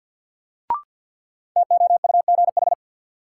Morse code, heard as a keyed tone, about 51 wpm, 700 Hz. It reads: TOUGH
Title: Sentences from Top 1200 Words - New Word Drill Alt - 50wpm